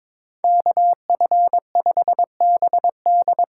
Morse code, heard as a keyed tone, about 22 wpm, 700 Hz. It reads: KF5BD
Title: USA Call Signs - 6 Characters - 8x Character Spacing - 22wpm